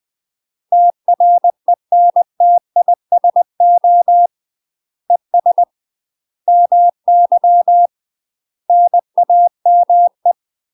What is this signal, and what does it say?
Morse code, heard as a keyed tone, about 20 wpm, 700 Hz. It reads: TRENTISO ES MY NAME
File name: UK MASTER QSOs TRAINING/20WPM